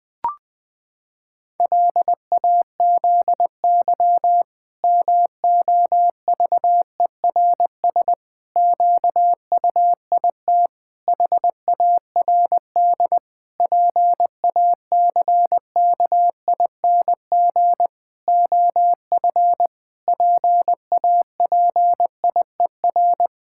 Morse code, heard as a keyed tone, about 20 wpm, 700 Hz. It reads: LAZY MOVERS QUIT HARD PACKING OF PAPIER
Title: Pangram Challenge v2 - 20wpm